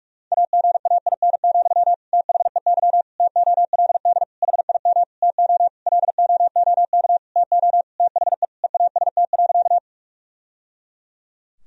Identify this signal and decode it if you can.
Morse code, heard as a keyed tone, about 46 wpm, 700 Hz. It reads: AGAIN, THEY TOLD HIM TO LOOK TO THE EAST.
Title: Sentences from Top 1000 Words - Review Part 3 - 45wpm